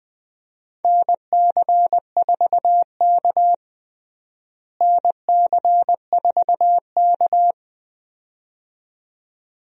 Morse code, heard as a keyed tone, about 20 wpm, 700 Hz. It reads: NC4K NC4K
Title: USA Call Signs - 4 Characters - Repeated 2x - 20wpm